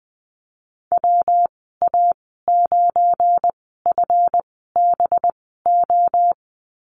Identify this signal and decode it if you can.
Morse code, heard as a keyed tone, about 20 wpm, spaced as for 14 wpm, 700 Hz. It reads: WA9FBO
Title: 186 Lesson 30g - Letter b - Call Signs 20wpm